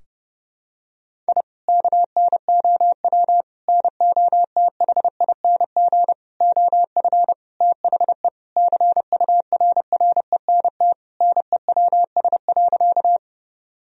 Morse code, heard as a keyed tone, about 30 wpm, 700 Hz. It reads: I KNOW NOTHING OF THE CURRENT NEWS.
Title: Sentences from Top 1000 Words - Review Part 4 - 30wpm